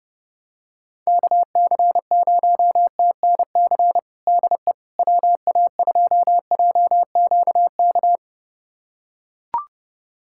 Morse code, heard as a keyed tone, about 32 wpm, 700 Hz. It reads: KC0TNC DE WA2JQK
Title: USA Callsign DE Callsign - 6 Characters - 30wpm